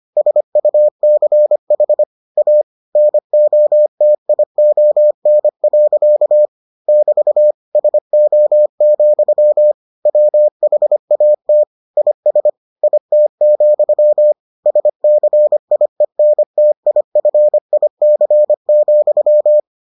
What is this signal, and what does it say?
Morse code, heard as a keyed tone, about 25 wpm, 600 Hz. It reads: SUCH A NOTION. <BT> SO, WHAT IS IT, SCIENTIFIC,